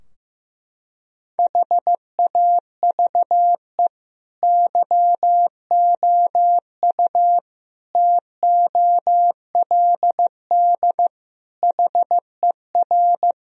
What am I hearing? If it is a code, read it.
Morse code, heard as a keyed tone, about 15 wpm, 700 Hz. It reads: HAVE YOU TOLD HER
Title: Sentences from Top 1000 Words - Review Part 2 - 15wpm